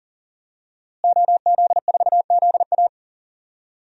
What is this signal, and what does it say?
Morse code, heard as a keyed tone, about 37 wpm, 700 Hz. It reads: OZ4ZA
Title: DXCC-100 Call Signs - 5 Characters - Repeated 2x - 40wpm